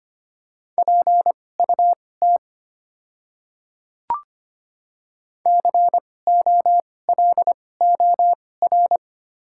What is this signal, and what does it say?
Morse code, heard as a keyed tone, about 25 wpm, spaced as for 18 wpm, 700 Hz. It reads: PUT COLOR
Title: 2x Character Spacing - Top 300 Words - 25wpm